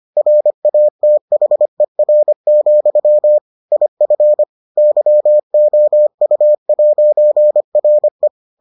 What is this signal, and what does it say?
Morse code, heard as a keyed tone, about 25 wpm, 600 Hz. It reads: RATHER, IF YOU'RE